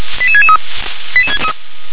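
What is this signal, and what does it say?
Five-tone selective call (ZVEI), heard as quick runs of five tones, tones of about 70 ms. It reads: 88663
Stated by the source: ZVEI five-tone selective call received on an SDR.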